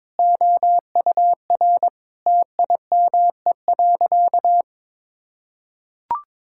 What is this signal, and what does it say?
Morse code, heard as a keyed tone, about 22 wpm, 700 Hz. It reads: OUR TIME.
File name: practice / 22wpm / Sentences with 4-Word Phrases - 22wpm